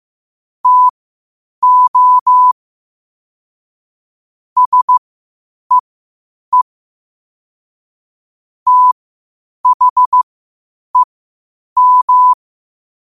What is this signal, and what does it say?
Morse code, heard as a keyed tone, about 15 wpm, spaced as for 8 wpm, 1000 Hz. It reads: TO SEE THEM
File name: qotdfiles / 8wpm